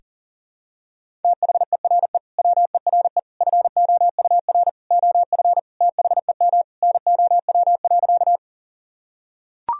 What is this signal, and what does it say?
Morse code, heard as a keyed tone, about 40 wpm, 700 Hz. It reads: THERE WERE FOUR OF THEM NOW.
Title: Sentences from Top 1000 Words - Review Part 1 - 40wpm